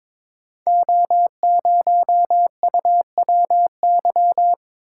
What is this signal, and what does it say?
Morse code, heard as a keyed tone, about 22 wpm, 700 Hz. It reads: O0UWY
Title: Code Groups - 5 Letter:Numbers - 22wpm